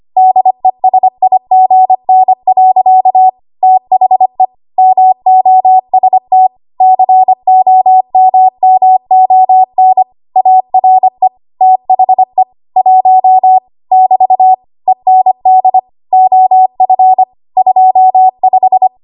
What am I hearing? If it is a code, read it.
Morse code, heard as a keyed tone, about 25 wpm, 750 Hz. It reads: DESIGN. THE MOST COMMON ARE THE 1 <BT> END OF 25
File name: Morse / 25WPM